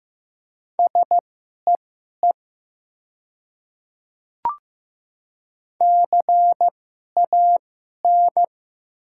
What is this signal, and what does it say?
Morse code, heard as a keyed tone, about 15 wpm, spaced as for 11 wpm, 700 Hz. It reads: SEE CAN
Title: 2x Character Spacing - Top 100 Words - 15wpm